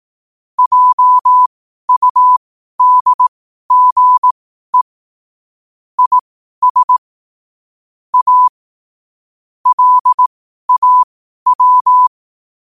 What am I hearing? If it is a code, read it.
Morse code, heard as a keyed tone, about 18 wpm, spaced as for 12 wpm, 1000 Hz. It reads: JUDGE IS A LAW
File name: qotdfiles / 12wpm